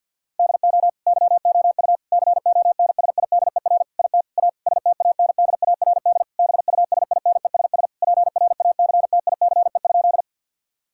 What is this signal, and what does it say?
Morse code, heard as a keyed tone, about 50 wpm, 700 Hz. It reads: DO YOU CONSIDER IT A STANDARD BUSINESS PRACTICE?
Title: Sentences from Top 1000 Words - Review Part 6 - 50wpm